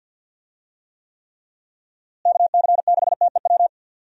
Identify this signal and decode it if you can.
Morse code, heard as a keyed tone, about 52 wpm, 700 Hz. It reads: KC6TEW
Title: USA Call Signs - 6 Characters - Repeated 3x - 50wpm